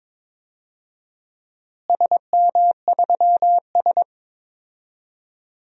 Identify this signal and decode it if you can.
Morse code, heard as a keyed tone, about 22 wpm, 700 Hz. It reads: SM3S
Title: DXCC-100 Call Signs - 4 Characters - Repeated 3x - 22wpm